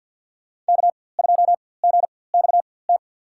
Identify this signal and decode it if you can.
Morse code, heard as a keyed tone, about 52 wpm, spaced as for 24 wpm, 700 Hz. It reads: K2GXT
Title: USA Call Signs - 5 Characters - 4x Character Spacing - 50wpm